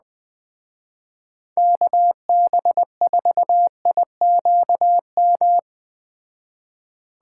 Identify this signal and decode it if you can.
Morse code, heard as a keyed tone, about 19 wpm, 700 Hz. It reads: KB4IQM
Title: Speed Racing - USA 2x3 Call Signs 17wpm